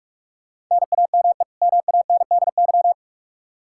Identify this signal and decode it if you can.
Morse code, heard as a keyed tone, about 42 wpm, 700 Hz. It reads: NAME MANDY
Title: Practice QSO - Exchange 1 - Beginning - 45wpm